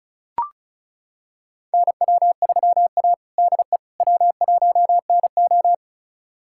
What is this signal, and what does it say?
Morse code, heard as a keyed tone, about 36 wpm, 700 Hz. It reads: NW3A DE W1NO
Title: USA Callsign DE Callsign - 4 Characters - 35wpm